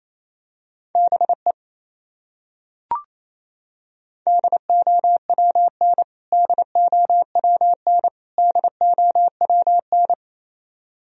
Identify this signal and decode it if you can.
Morse code, heard as a keyed tone, about 28 wpm, 700 Hz. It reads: BE DOWN DOWN DOWN
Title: Top 100 Words Repeated 28wpm